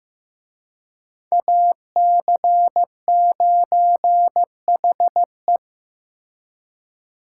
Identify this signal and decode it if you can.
Morse code, heard as a keyed tone, about 18 wpm, 700 Hz. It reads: AC9HE
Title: Speed Racing - USA 2x2 Call Signs 15wpm